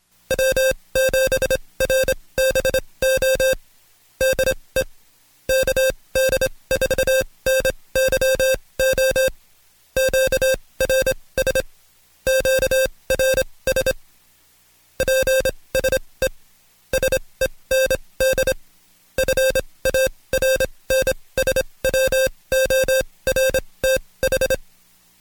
Morse code, heard as a keyed tone, about 26 wpm, spaced as for 20 wpm, 530 Hz, with noise. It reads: W7RBO DE KD4NYO QRS QRS PSE SEND FARNSWORTH
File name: Code Quick / AudioDownld2 / Seg136-Stepper9 19WPM